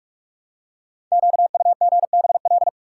Morse code, heard as a keyed tone, about 44 wpm, 700 Hz. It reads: QUGBL
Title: Code Groups - 5 Letters - 45wpm